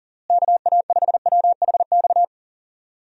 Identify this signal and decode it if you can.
Morse code, heard as a keyed tone, about 40 wpm, 700 Hz. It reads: KA5WHX
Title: USA Call Signs - 6 Characters - Repeated 3x - 40wpm